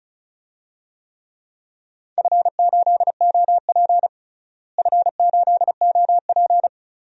Morse code, heard as a keyed tone, about 36 wpm, 700 Hz. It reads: F8OP F8OP
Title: DXCC-100 Call Signs - 4 Characters - Repeated 3x - 35wpm